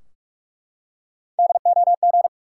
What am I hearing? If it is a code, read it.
Morse code, heard as a keyed tone, about 44 wpm, 700 Hz. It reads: DOG
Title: Three Letter Words - 45wpm